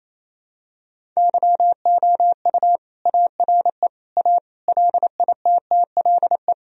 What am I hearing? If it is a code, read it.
Morse code, heard as a keyed tone, about 28 wpm, 700 Hz. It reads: YOU ARE A LITTLE
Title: Sentences from Top 1000 Words - Review Part 1 - 28wpm